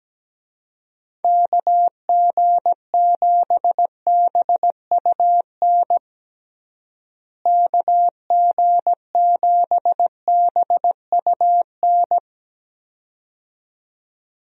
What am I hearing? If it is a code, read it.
Morse code, heard as a keyed tone, about 17 wpm, 700 Hz. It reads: KG7BUN KG7BUN
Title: USA Call Signs - 6 Characters - Repeated 2x - 17wpm